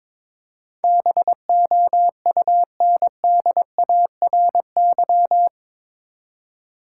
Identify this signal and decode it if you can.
Morse code, heard as a keyed tone, about 22 wpm, 700 Hz. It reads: BOUNDARY
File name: practice / 22wpm / Eight Letter Words - 22wpm